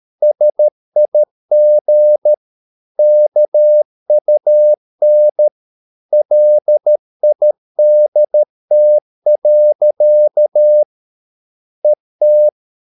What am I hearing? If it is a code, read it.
Morse code, heard as a keyed tone, about 13 wpm, 600 Hz. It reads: SIG KUN LIDT. ET
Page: Ferie 13 wpm